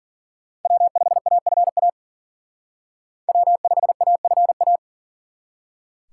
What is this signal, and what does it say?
Morse code, heard as a keyed tone, about 42 wpm, 700 Hz. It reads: W5AFA W5AFA
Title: Speed Racing - USA 1x3 Call Signs 40wpm